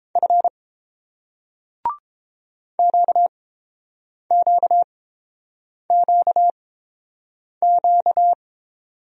Morse code, heard as a keyed tone, about 29 wpm, spaced as for 7 wpm, 700 Hz. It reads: F QQQQ